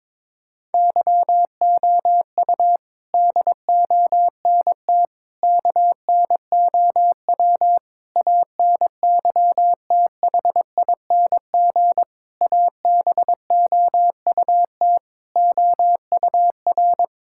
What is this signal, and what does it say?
Morse code, heard as a keyed tone, about 22 wpm, 700 Hz. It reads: YOU DONT KNOW ANYTHING ABOUT OUR
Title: Sentences from Top 1000 Words - Review Part 6 - 22wpm